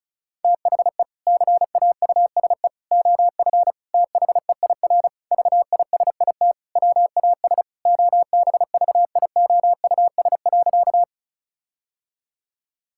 Morse code, heard as a keyed tone, about 36 wpm, 700 Hz. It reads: THE CAUSE OF THEIR VISIT WAS OBVIOUS.
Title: Sentences from Top 1600 Words 35wpm